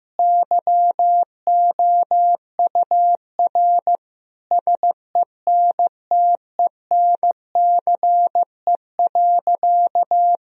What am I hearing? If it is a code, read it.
Morse code, heard as a keyed tone, about 16 wpm, 700 Hz. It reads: YOUR SENTENCE.
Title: Sentences from Top 1000 Words - Review Part 1 - 15wpm